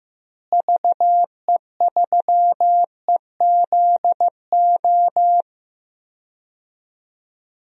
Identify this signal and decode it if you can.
Morse code, heard as a keyed tone, about 15 wpm, 700 Hz. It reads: VE3EZO
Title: DXCC-100 Call Signs - 6 Characters - Repeated 3x - 15wpm